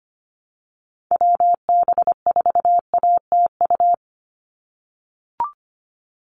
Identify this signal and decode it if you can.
Morse code, heard as a keyed tone, about 25 wpm, 700 Hz. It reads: WB4ATU
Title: USA Call Signs 25wpm